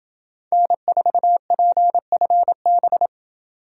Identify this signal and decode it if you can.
Morse code, heard as a keyed tone, about 28 wpm, 700 Hz. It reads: N4PFB